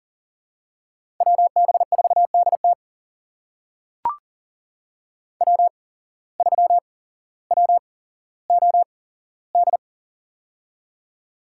Morse code, heard as a keyed tone, about 40 wpm, spaced as for 17 wpm, 700 Hz. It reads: WB4DT W3WOD
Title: USA Call Signs - 5 Characters - 8x Character Spacing - 40wpm